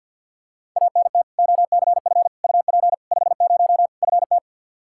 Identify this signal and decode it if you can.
Morse code, heard as a keyed tone, about 52 wpm, 700 Hz. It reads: ANT OCF UP 50 FT
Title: Practice QSO - Exchange 2 - Info - 50wpm